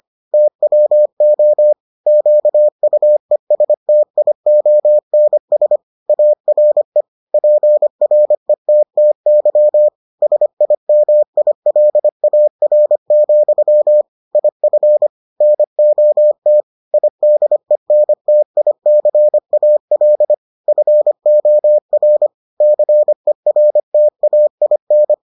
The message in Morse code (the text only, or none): TWO QUESTIONS ARE PRETTY SIMILAR, IF NOT IDENTICAL FOR CERTAIN